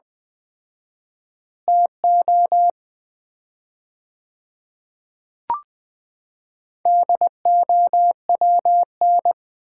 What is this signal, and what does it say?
Morse code, heard as a keyed tone, about 20 wpm, 700 Hz. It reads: TO DOWN
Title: 4x Word-Spacing - Sets of 2 Words - 20wpm